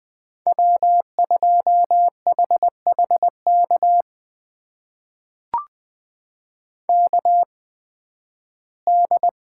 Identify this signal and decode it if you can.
Morse code, heard as a keyed tone, about 20 wpm, spaced as for 13 wpm, 700 Hz. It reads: W2HHK KD
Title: USA Call Signs - 5 Characters - 8x Character Spacing - 20wpm